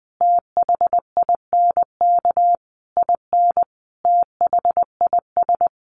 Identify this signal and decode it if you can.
Morse code, heard as a keyed tone, about 20 wpm, 700 Hz. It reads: THINK IN THIS
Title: Sentences from Top 1200 Words 20wpm